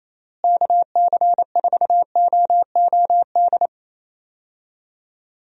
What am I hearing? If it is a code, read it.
Morse code, heard as a keyed tone, about 28 wpm, 700 Hz. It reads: KC4OOD
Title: USA Call Signs - 6 Characters - Repeated 3x - 28wpm